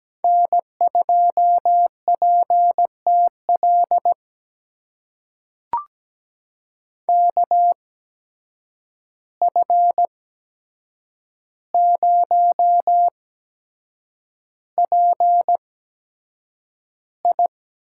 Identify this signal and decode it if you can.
Morse code, heard as a keyed tone, about 17 wpm, spaced as for 7 wpm, 700 Hz. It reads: N2PTL KF0PI
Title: USA Call Signs - 5 Characters - 8x Character Spacing - 17wpm